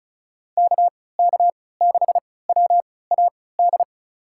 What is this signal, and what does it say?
Morse code, heard as a keyed tone, about 36 wpm, spaced as for 20 wpm, 700 Hz. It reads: KK6WAD
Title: USA Call Signs - 6 Characters - 3x Character Spacing - 35wpm